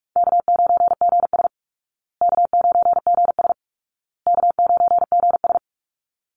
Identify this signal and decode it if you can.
Morse code, heard as a keyed tone, about 44 wpm, 700 Hz. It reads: K9GS K9GS K9GS
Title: Call Signs Repeated 2x - 45wpm